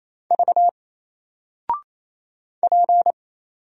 Morse code, heard as a keyed tone, about 27 wpm, 700 Hz. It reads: V P
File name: Single-Letter-Number-28wpm